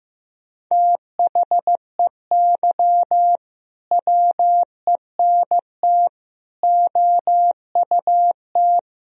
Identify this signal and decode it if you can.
Morse code, heard as a keyed tone, about 15 wpm, 700 Hz. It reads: THEY WENT OUT
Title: Sentences from Top 1000 Words - Review Part 3 - 15wpm